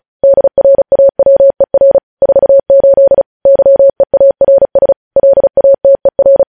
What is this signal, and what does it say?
Morse code, heard as a keyed tone, about 36 wpm, 550 Hz. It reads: DRAWER 48 YEARS LATER